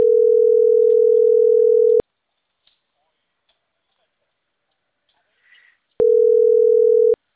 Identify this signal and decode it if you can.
Ringback tone (North America), 440 + 480 Hz.